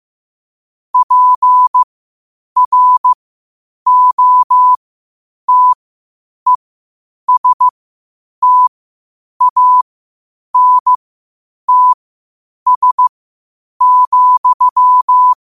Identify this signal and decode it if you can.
Morse code, heard as a keyed tone, about 15 wpm, spaced as for 8 wpm, 1000 Hz. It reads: PROTESTANTS,
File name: qotdfiles / 8wpm